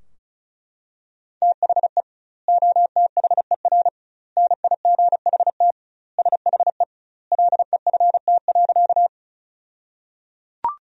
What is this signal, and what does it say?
Morse code, heard as a keyed tone, about 36 wpm, 700 Hz. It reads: THE OTHER NIGHT SHE LEFT.
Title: 2x Word-Spacing - Sentences from Top 300 Words - 35wpm